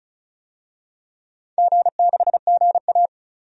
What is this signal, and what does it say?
Morse code, heard as a keyed tone, about 36 wpm, 700 Hz. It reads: G6GA
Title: DXCC-100 Call Signs - 4 Characters - Repeated 2x - 35wpm